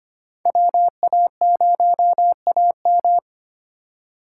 Morse code, heard as a keyed tone, about 25 wpm, 700 Hz. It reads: WA0AM
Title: USA Call Signs - 5 Characters - 4x Character Spacing - 25wpm